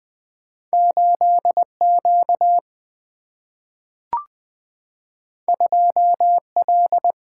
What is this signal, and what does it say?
Morse code, heard as a keyed tone, about 20 wpm, 700 Hz. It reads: 8 Q 2 L
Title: Single-Letter Single-Number 20wpm